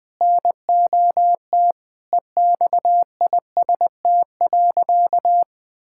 Morse code, heard as a keyed tone, about 20 wpm, 700 Hz. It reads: NOT EXIST.
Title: Sentences from Top 1000 Words - Review Part 6 - 20wpm